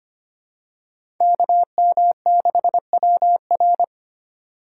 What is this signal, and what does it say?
Morse code, heard as a keyed tone, about 25 wpm, 700 Hz. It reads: KM6WR
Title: USA Call Signs - 3x Character Spacing - 25wpm